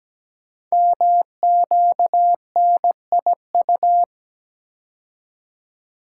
Morse code, heard as a keyed tone, about 17 wpm, 700 Hz. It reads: MQNIU
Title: Code Groups - 5 Letters - 17wpm